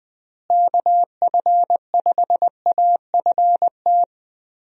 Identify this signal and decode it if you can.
Morse code, heard as a keyed tone, about 19 wpm, 700 Hz. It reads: KF5AFT
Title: USA Call Signs - 6 Characters - Repeated 3x - 20wpm